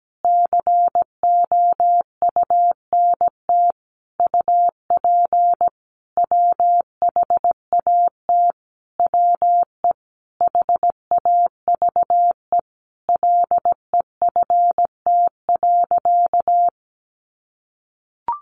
Morse code, heard as a keyed tone, about 17 wpm, 700 Hz. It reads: COUNT UP WHAT WE HAVE LEFT.
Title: Sentences from Top 1500 Words 17wpm